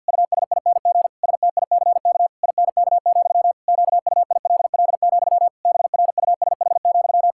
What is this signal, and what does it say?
Morse code, heard as a keyed tone, about 49 wpm, 700 Hz. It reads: USING STICK INK, QUILL, BRUSH,
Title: Pangram Challenge v2 - 50wpm